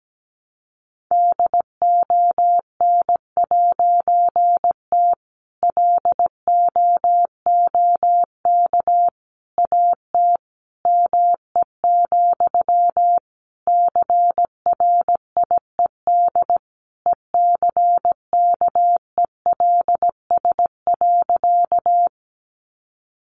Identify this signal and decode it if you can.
Morse code, heard as a keyed tone, about 17 wpm, 700 Hz. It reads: DON'T LOOK AT ME, CRIED ECKELS.
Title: A Sound Of Thunder 17wpm